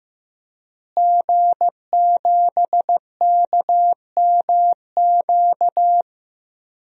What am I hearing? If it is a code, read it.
Morse code, heard as a keyed tone, about 15 wpm, 700 Hz. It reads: G7KMQ
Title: DXCC-100 Call Signs - 5 Characters - Repeated 2x - 15wpm